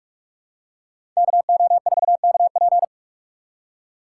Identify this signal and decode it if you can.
Morse code, heard as a keyed tone, about 45 wpm, 700 Hz. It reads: KO4KP